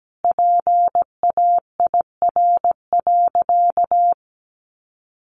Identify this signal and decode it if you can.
Morse code, heard as a keyed tone, about 17 wpm, 700 Hz. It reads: PAIR.
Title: Sentences from Top 1200 Words 17wpm